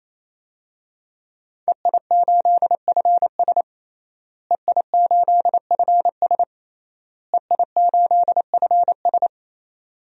Morse code, heard as a keyed tone, about 28 wpm, 700 Hz. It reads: EI8FS EI8FS EI8FS
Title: DXCC-100 Call Signs - 5 Characters - Repeated 3x - 28wpm